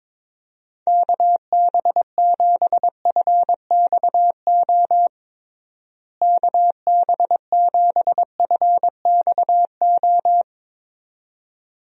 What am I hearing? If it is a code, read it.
Morse code, heard as a keyed tone, about 22 wpm, 700 Hz. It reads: KB7FXO KB7FXO
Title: USA Call Signs - 6 Characters - Repeated 2x - 22wpm